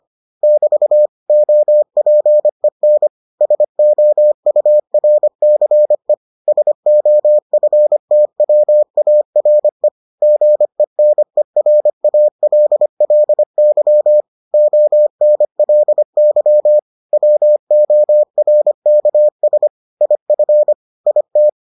Morse code, heard as a keyed tone, about 25 wpm, 600 Hz. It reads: <BT> OPEN SOURCE SOFTWARE GENERALLY ONLY WORKS IF IT